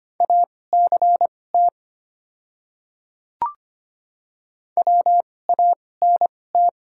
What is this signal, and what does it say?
Morse code, heard as a keyed tone, about 25 wpm, spaced as for 18 wpm, 700 Hz. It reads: ACT WANT
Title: 2x Character Spacing - Top 300 Words - 25wpm